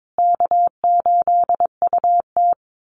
Morse code, heard as a keyed tone, about 22 wpm, 700 Hz. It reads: K8UT
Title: Call Signs Repeated 3x - 22wpm